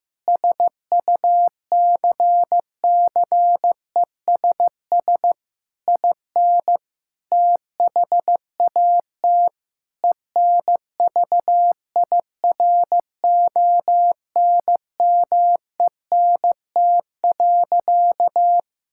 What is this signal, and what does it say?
Morse code, heard as a keyed tone, about 15 wpm, 700 Hz. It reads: SUCCESS IN THAT ENVIRONMENT.
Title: Sentences from Top 1000 Words - Review Part 5 - 15wpm